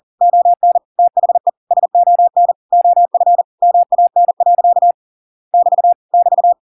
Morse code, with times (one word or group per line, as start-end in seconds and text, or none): 0.2-0.8 ON
1.0-1.5 THE
1.7-2.5 SON
2.7-3.4 OF
3.6-4.9 MAN.
5.5-5.9 <BT>
6.1-6.5 <BT>